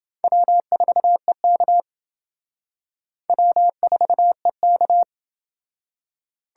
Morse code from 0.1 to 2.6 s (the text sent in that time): W4EK